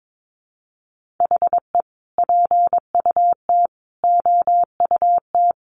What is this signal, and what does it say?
Morse code, heard as a keyed tone, about 22 wpm, 700 Hz. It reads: HE PUT OUT
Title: Sentences from Top 800 Words 22wpm